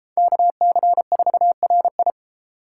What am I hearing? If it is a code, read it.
Morse code, heard as a keyed tone, about 37 wpm, 700 Hz. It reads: KC4RI